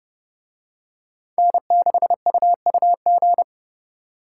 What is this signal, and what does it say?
Morse code, heard as a keyed tone, about 29 wpm, 700 Hz. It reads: N6UUG